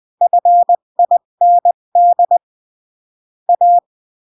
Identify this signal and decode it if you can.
Morse code, heard as a keyed tone, about 20 wpm, 700 Hz. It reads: FIND A
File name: CW / jodorowsky / 82 commandments 20-15